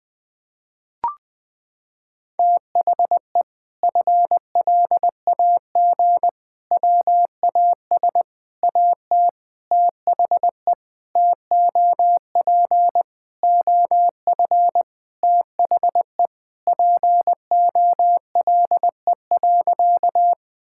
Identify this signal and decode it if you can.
Morse code, heard as a keyed tone, about 21 wpm, 700 Hz. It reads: THE FLAG WAS AT THE TOP OF THE POLE.
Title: Sentences with 5-Word Phrases - 20wpm